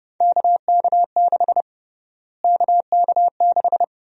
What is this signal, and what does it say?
Morse code, heard as a keyed tone, about 30 wpm, 700 Hz. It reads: KK6 KK6
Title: USA Call Sign Prefixes - Repeated 2x - 30wpm